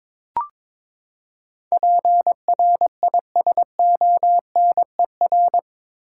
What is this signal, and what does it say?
Morse code, heard as a keyed tone, about 22 wpm, 700 Hz. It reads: PRISONER
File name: practice / 22wpm / Eight Letter Words - 22wpm